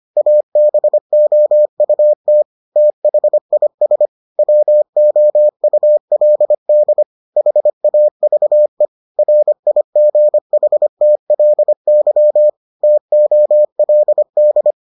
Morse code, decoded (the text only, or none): ABOUT THIS WOULD HAVE RIGHTLY TOLD